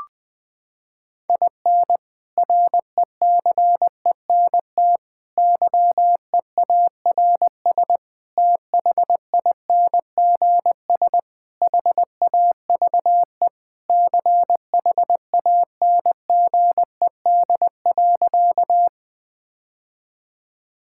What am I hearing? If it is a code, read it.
Morse code, heard as a keyed tone, about 21 wpm, 700 Hz. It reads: IN RECENT YEARS THINGS HAVE CHANGED.
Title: Sentences with 3-Word Phrases - 20wpm